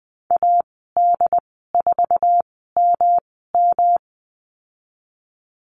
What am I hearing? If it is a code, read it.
Morse code, heard as a keyed tone, about 20 wpm, spaced as for 15 wpm, 700 Hz. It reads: AD4MM
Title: 128 Lesson 21g - Number 6 - Call Signs 20wpm